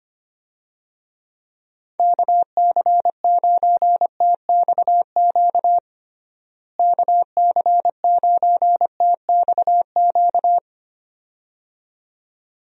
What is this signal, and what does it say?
Morse code, heard as a keyed tone, about 25 wpm, 700 Hz. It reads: KC9TXQ KC9TXQ
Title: USA Call Signs - 6 Characters - Repeated 2x - 25wpm